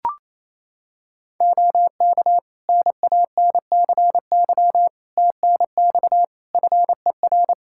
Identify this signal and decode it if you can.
Morse code, heard as a keyed tone, about 28 wpm, 700 Hz. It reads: OK NANCY TNX FER
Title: Practice QSO - Exchange 4 - Ending - 28wpm